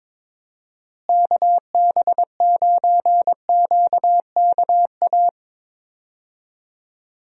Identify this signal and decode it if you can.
Morse code, heard as a keyed tone, about 22 wpm, 700 Hz. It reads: KB9QKA
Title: USA Call Signs - 6 Characters - Repeated 2x - 22wpm